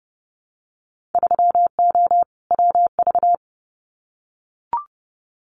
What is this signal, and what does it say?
Morse code, heard as a keyed tone, about 30 wpm, 700 Hz. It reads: 3O WV
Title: Winter Field Day Exchanges 30wpm